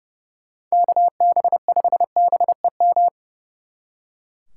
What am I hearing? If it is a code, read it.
Morse code, heard as a keyed tone, about 36 wpm, 700 Hz. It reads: KB5BEM
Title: Speed Racing - USA 2x3 Call Signs 30wpm